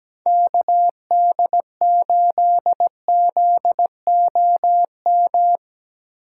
Morse code, heard as a keyed tone, about 17 wpm, 700 Hz. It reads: KD8ZOM
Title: USA Call Signs - 8x Character Spacing - 17wpm